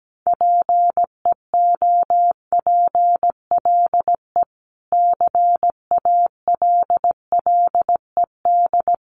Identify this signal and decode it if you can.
Morse code, heard as a keyed tone, about 17 wpm, 700 Hz. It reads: PEOPLE CALLED
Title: A Sound Of Thunder 17wpm